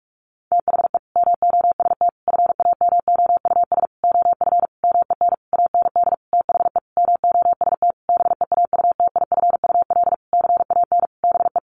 Morse code, heard as a keyed tone, about 44 wpm, 700 Hz. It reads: THE MOST FAMOUS OF MEN AND THE MOST BEAUTIFUL CAN BE
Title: Sentences from Top 1500 Words 45wpm